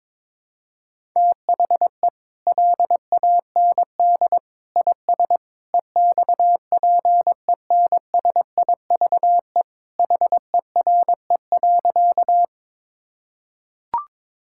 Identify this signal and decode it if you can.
Morse code, heard as a keyed tone, about 22 wpm, 700 Hz. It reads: THE LAND IS EXPENSIVE HERE.
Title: Sentences from Top 1000 Words - Review Part 6 - 22wpm